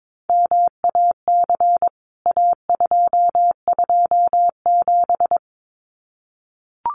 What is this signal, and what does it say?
Morse code, heard as a keyed tone, about 22 wpm, 700 Hz. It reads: MAC A227
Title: CWT Exchange 22wpm